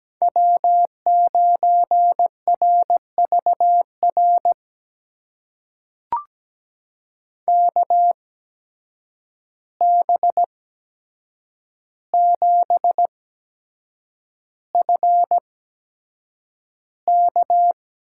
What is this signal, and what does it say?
Morse code, heard as a keyed tone, about 17 wpm, spaced as for 7 wpm, 700 Hz. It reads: W9RVR KB7FK
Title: USA Call Signs - 5 Characters - 8x Character Spacing - 17wpm